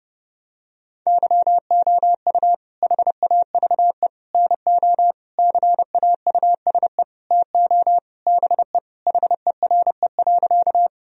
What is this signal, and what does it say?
Morse code, heard as a keyed tone, about 32 wpm, 700 Hz. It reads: YOU HAVE NO CAUSE TO BE HERE.
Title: Sentences from Top 1000 Words - Review Part 1 - 30wpm